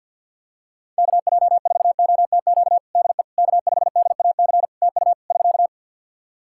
Morse code, heard as a keyed tone, about 49 wpm, 700 Hz. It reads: KJ4OTQ DE K5DAC TU <SK>
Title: Practice QSO - Exchange 4 - Ending - 50wpm